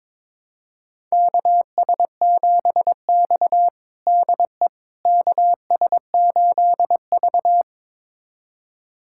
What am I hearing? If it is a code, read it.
Morse code, heard as a keyed tone, about 22 wpm, 700 Hz. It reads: KS7X DE KS8V
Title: USA Callsign DE Callsign - 4 Characters - 22wpm